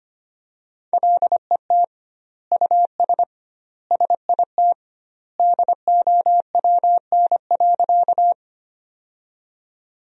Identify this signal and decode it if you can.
Morse code, heard as a keyed tone, about 25 wpm, 700 Hz. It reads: LET US SIT DOWN.
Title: 2x Word-Spacing - Sentences from Top 500 Words - 25wpm